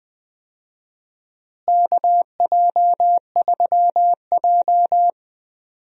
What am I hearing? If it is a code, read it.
Morse code, heard as a keyed tone, about 20 wpm, 700 Hz. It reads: KJ3J
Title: USA Call Signs - 4 Characters - Repeated 3x - 20wpm